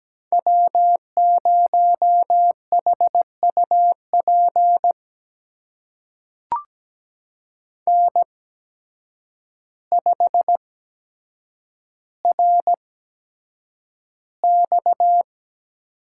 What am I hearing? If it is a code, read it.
Morse code, heard as a keyed tone, about 17 wpm, spaced as for 8 wpm, 700 Hz. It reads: W0HUP N5RX
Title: USA Call Signs - 5 Characters - 8x Character Spacing - 17wpm